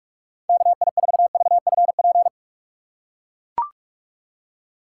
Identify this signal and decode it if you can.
Morse code, heard as a keyed tone, about 43 wpm, 700 Hz. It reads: KI4VFP
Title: USA Call Signs - 6 Characters - 8x Character Spacing - 45wpm